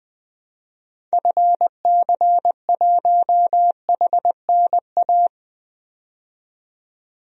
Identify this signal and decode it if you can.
Morse code, heard as a keyed tone, about 20 wpm, 700 Hz. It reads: FC1HNA
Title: DXCC-100 Call Signs - 6 Characters - 20wpm